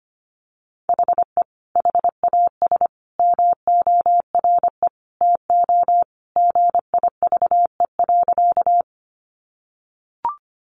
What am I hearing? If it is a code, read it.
Morse code, heard as a keyed tone, about 25 wpm, 700 Hz. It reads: HE HAS MORE TO GIVE.
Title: Sentences from Top 200 Words 25wpm